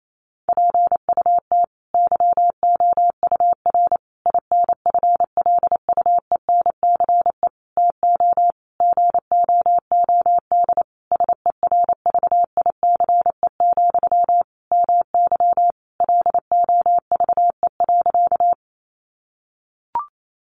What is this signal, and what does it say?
Morse code, heard as a keyed tone, about 28 wpm, 700 Hz. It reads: PUT YOUR INFLUENCE TO GOOD SERVICE, MY LOVE.
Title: Sentences from Top 1100 Words 28wpm